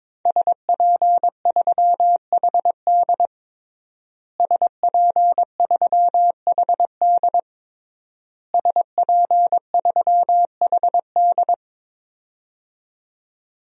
Morse code, heard as a keyed tone, about 22 wpm, 700 Hz. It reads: SP3HD SP3HD SP3HD
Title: DXCC-100 Call Signs - 5 Characters - Repeated 3x - 22wpm